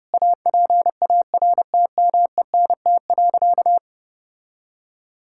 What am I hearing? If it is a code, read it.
Morse code, heard as a keyed tone, about 30 wpm, 700 Hz. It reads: APARTMENT.
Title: Sentences from Top 1000 Words - Review Part 6 - 30wpm